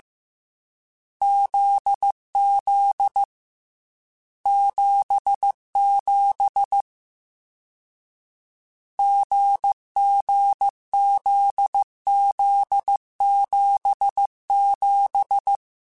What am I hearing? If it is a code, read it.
Morse code, heard as a keyed tone, about 15 wpm, 800 Hz. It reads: ZZ 77 GGZZ77